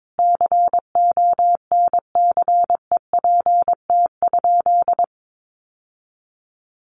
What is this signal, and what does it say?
Morse code, heard as a keyed tone, about 22 wpm, 700 Hz. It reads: CONCEPT?
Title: Sentences from Top 1100 Words 22wpm